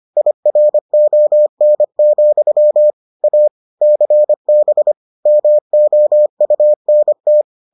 Morse code, heard as a keyed tone, about 25 wpm, 600 Hz. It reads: IRON, A CB MOUNT